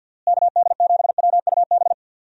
Morse code, heard as a keyed tone, about 50 wpm, 700 Hz. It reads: KD7WUB
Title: USA Call Signs - 6 Characters - Repeated 3x - 50wpm